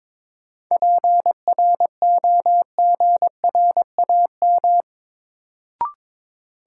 Morse code, heard as a keyed tone, about 22 wpm, 700 Hz. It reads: PROGRAM
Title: Sentences from Top 1500 Words - New Word Drill Alt - 22wpm